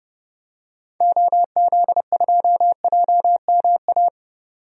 Morse code, heard as a keyed tone, about 32 wpm, 700 Hz. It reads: OZ2JMA
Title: DXCC-100 Call Signs - 6 Characters - Repeated 3x - 30wpm